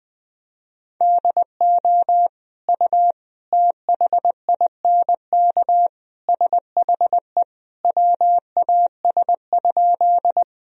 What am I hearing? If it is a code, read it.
Morse code, heard as a keyed tone, about 21 wpm, 700 Hz. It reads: DO U THINK SHE WAS?
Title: Sentences from Top 200 Words and Abbreviations 20wpm